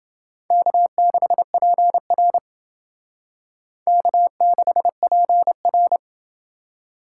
Morse code, heard as a keyed tone, about 28 wpm, 700 Hz. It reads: K6PR K6PR